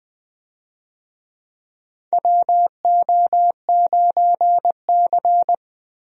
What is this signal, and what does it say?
Morse code, heard as a keyed tone, about 20 wpm, 700 Hz. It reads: WO9C
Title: USA Call Signs - 4 Characters - Repeated 3x - 20wpm